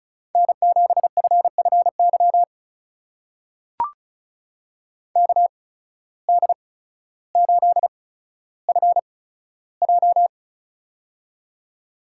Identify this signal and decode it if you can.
Morse code, heard as a keyed tone, about 36 wpm, spaced as for 15 wpm, 700 Hz. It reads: N7FFY KD8FJ
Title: USA Call Signs - 8x Character Spacing - 35wpm